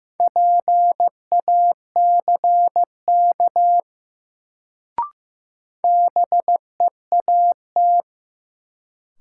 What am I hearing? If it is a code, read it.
Morse code, heard as a keyed tone, about 15 wpm, 700 Hz. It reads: PACK BEAT
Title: Sentences from Top 1300 Words - New Word Drill Alt - 15wpm